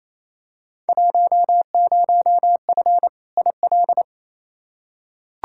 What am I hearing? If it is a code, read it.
Morse code, heard as a keyed tone, about 28 wpm, 700 Hz. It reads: 10F IL
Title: Field Day Exchanges Encore - 28wpm